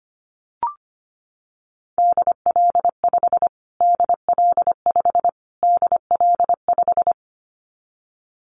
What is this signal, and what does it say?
Morse code, heard as a keyed tone, about 25 wpm, 700 Hz. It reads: DL5 DL5 DL5
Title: Callsign Prefixes for Top 25 DXCC - Repeated 3x - 25wpm